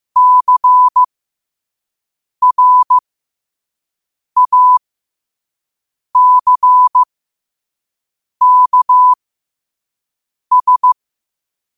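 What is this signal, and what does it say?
Morse code, heard as a keyed tone, about 15 wpm, spaced as for 5 wpm, 1000 Hz. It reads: CRACKS
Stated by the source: Morse code sent at 5 words per minute.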